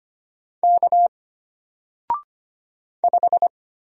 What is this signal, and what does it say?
Morse code, heard as a keyed tone, about 25 wpm, 700 Hz. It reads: K 5
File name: Single-Letter-Number-25wpm